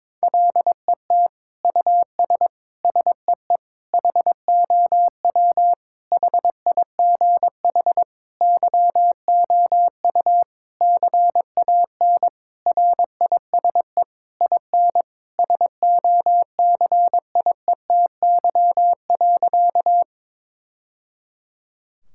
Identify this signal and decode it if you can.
Morse code, heard as a keyed tone, about 21 wpm, 700 Hz. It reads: LET US SEE HOW HIGH YOU CAN RISE IN SOCIETY.
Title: Sentences from Top 1000 Words - Review Part 6 - 22wpm